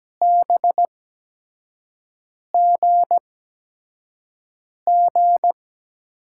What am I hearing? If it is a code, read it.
Morse code, heard as a keyed tone, about 17 wpm, spaced as for 5 wpm, 700 Hz. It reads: BGG
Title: USA Call Signs - 8x Character Spacing - 17wpm